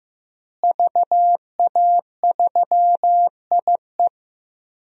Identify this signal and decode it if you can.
Morse code, heard as a keyed tone, about 15 wpm, 700 Hz. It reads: VA3IE
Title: DXCC-100 Call Signs - 5 Characters - Repeated 2x - 15wpm